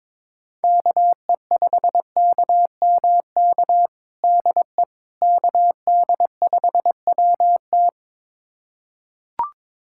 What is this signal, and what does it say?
Morse code, heard as a keyed tone, about 22 wpm, 700 Hz. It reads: KE5KMK DE KD5WT
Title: USA Callsign DE Callsign 22wpm